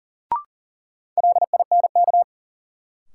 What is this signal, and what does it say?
Morse code, heard as a keyed tone, about 40 wpm, 700 Hz. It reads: LINK
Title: Sentences from Top 1000 Words - Review Part 7 - New Word Drill - 40wpm